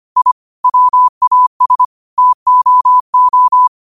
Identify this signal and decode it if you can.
Morse code, heard as a keyed tone, about 25 wpm, 1000 Hz. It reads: I WAS TOO